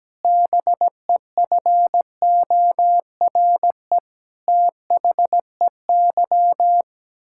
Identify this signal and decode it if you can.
Morse code, heard as a keyed tone, about 17 wpm, 700 Hz. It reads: BEFORE THEY
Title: Pangram Challenge v2 - 17wpm